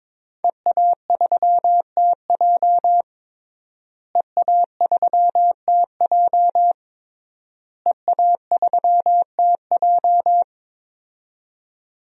Morse code, heard as a keyed tone, about 22 wpm, 700 Hz. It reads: EA3TJ EA3TJ EA3TJ
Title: DXCC-100 Call Signs - 5 Characters - Repeated 3x - 22wpm